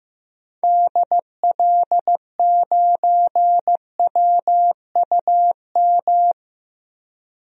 Morse code, heard as a keyed tone, about 16 wpm, 700 Hz. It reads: DL9WUM
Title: DXCC-100 Call Signs - 6 Characters - Repeated 2x - 15wpm